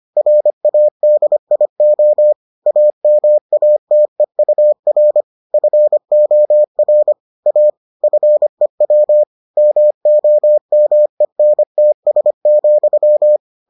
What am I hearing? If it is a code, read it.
Morse code, heard as a keyed tone, about 25 wpm, 600 Hz. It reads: RADIO AMATEUR FOR A FEW MOMENTS,